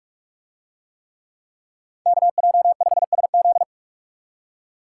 Morse code, heard as a keyed tone, about 39 wpm, 700 Hz. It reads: KJ5SZ